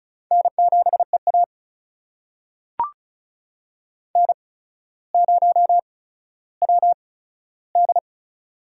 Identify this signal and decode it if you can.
Morse code, heard as a keyed tone, about 36 wpm, spaced as for 15 wpm, 700 Hz. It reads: N7EA N0WD
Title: USA Call Signs - 4 Characters - 8x Character Spacing - 35wpm